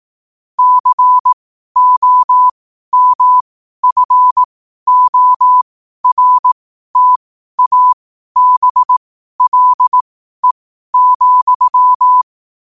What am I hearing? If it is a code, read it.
Morse code, heard as a keyed tone, about 18 wpm, spaced as for 12 wpm, 1000 Hz. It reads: COMFORTABLE,
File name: qotdfiles / 12wpm